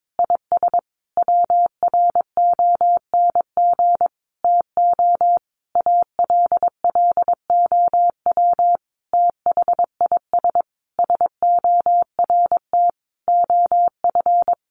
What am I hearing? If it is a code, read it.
Morse code, heard as a keyed tone, about 22 wpm, 700 Hz. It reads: IS WRONG TO ALLOW THIS SORT OF
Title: Sentences from Top 1300 Words 22wpm